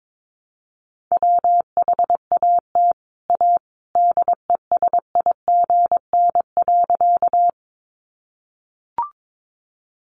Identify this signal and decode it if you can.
Morse code, heard as a keyed tone, about 22 wpm, 700 Hz. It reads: WHAT A DESIGN.
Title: Sentences from Top 800 Words 22wpm